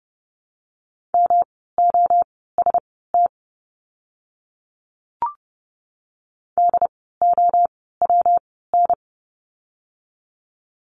Morse code, heard as a keyed tone, about 30 wpm, spaced as for 17 wpm, 700 Hz. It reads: MOST DOWN
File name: course / 30:10wpm / 237 Lesson 38f - Letter q - Words 30wpm